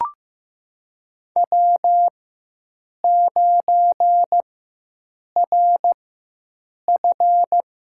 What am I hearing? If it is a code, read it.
Morse code, heard as a keyed tone, about 15 wpm, spaced as for 7 wpm, 700 Hz. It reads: W9RF
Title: USA Call Signs - 4 Characters - 4x Character Spacing - 15wpm